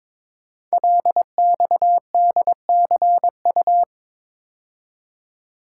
Morse code, heard as a keyed tone, about 22 wpm, 700 Hz. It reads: LXDCU
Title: Code Groups - 5 Letters - 22wpm